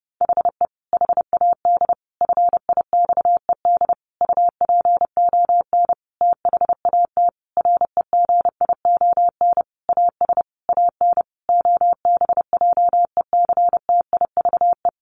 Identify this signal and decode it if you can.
Morse code, heard as a keyed tone, about 30 wpm, 700 Hz. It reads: HE HAD FIXED UPON THAT REGION AS AN OBJECTIVE
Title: Sentences from Top 1500 Words 30wpm